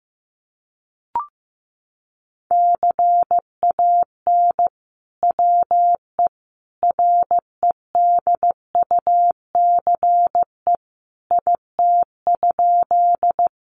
Morse code, heard as a keyed tone, about 15 wpm, 700 Hz. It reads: CAN WE REDUCE IT?
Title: Sentences from Top 1100 Words 15wpm